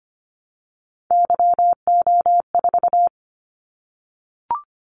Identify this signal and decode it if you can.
Morse code, heard as a keyed tone, about 25 wpm, 700 Hz. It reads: YO4
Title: Callsign Prefixes for Top 25 DXCC - 25wpm